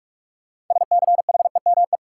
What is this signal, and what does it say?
Morse code, heard as a keyed tone, about 46 wpm, 700 Hz. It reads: SCHEME
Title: Six Letter Words - 45wpm